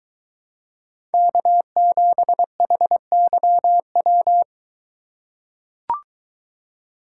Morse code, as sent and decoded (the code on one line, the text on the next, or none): -.- --... .... -.-- .--
K7HYW